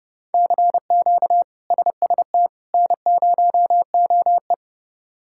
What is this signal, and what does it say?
Morse code, heard as a keyed tone, about 30 wpm, 700 Hz. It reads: CQ SST N0OE
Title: K1USN SST QSO Practice - 30wpm